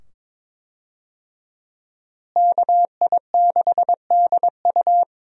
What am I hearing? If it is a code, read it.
Morse code, heard as a keyed tone, about 22 wpm, 700 Hz. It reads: KI6DU
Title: USA Call Signs - 5 Characters - Repeated 2x - 22wpm